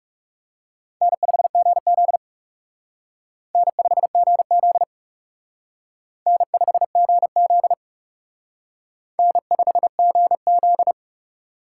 Morse code, heard as a keyed tone, about 37 wpm, 700 Hz. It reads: N5GZ N5GZ N5GZ N5GZ
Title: Speed Racing - USA 1x2 Call Signs 30wpm